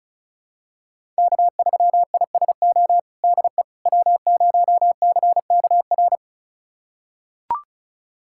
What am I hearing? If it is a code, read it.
Morse code, heard as a keyed tone, about 34 wpm, 700 Hz. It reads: K3ISO DE W0CKR
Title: USA Callsign DE Callsign - 5 Characters - 35wpm